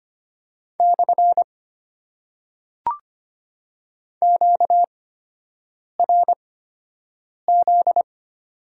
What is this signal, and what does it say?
Morse code, heard as a keyed tone, about 25 wpm, spaced as for 7 wpm, 690 Hz. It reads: / QRZ
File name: Top 90 QSO Elements - 8x Character Spacing - 25wpm